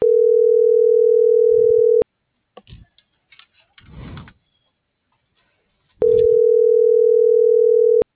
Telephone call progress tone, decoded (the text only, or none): ringback tone